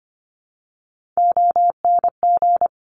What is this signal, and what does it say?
Morse code, heard as a keyed tone, about 25 wpm, 700 Hz. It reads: ONG
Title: Top Three Letter Combinations in Words 25wpm